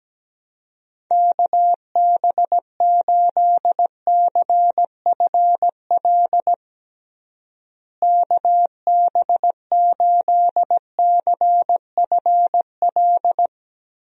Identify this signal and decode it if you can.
Morse code, heard as a keyed tone, about 17 wpm, 700 Hz. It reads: KB8CFL KB8CFL
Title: USA Call Signs - 6 Characters - Repeated 2x - 17wpm